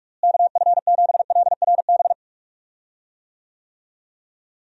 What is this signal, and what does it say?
Morse code, heard as a keyed tone, about 46 wpm, 700 Hz. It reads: KF7LRB
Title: USA Call Signs - 6 Characters - Repeated 2x - 45wpm